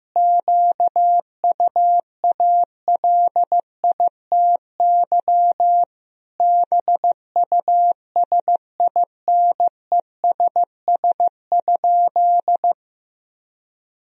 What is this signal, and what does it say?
Morse code, heard as a keyed tone, about 15 wpm, 700 Hz. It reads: QUALITY BUSINESS?
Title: Sentences from Top 1000 Words - Review Part 4 - 15wpm